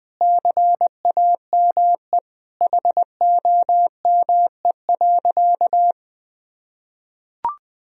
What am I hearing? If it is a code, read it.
Morse code, heard as a keyed tone, about 20 wpm, 700 Hz. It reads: CAME HOME.
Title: Sentences from Top 1000 Words - Review Part 4 - 20wpm